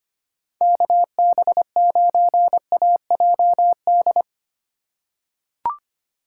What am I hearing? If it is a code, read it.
Morse code, heard as a keyed tone, about 25 wpm, 700 Hz. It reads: KB9AJD
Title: USA Call Signs - 8x Character Spacing - 25wpm